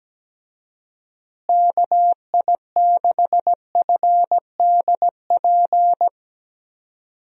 Morse code, heard as a keyed tone, about 17 wpm, 700 Hz. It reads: KI6FDP
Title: USA Call Signs - 6 Characters - Repeated 3x - 17wpm